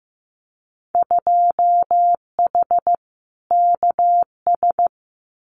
Morse code, heard as a keyed tone, about 15 wpm, 700 Hz. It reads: 2H KS
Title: Winter Field Day Exchanges Encore 15wpm